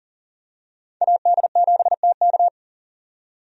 Morse code, heard as a keyed tone, about 40 wpm, 700 Hz. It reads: AD7TK
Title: USA Call Signs - 5 Characters - 40wpm